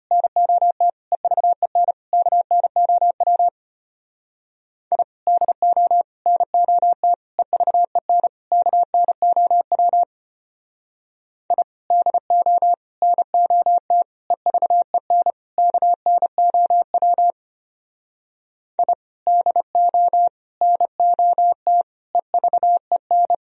Morse code, heard as a keyed tone, about 31 wpm, 700 Hz. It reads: NOT EVEN KNOW I DO NOT EVEN KNOW I DO NOT EVEN KNOW I DO NOT EVEN
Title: Speed-Racing - Sets of 5 Words 25wpm